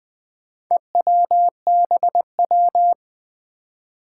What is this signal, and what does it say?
Morse code, heard as a keyed tone, about 20 wpm, 700 Hz. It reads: EWBW
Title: Four Challenging Letter-Number Combinations 20wpm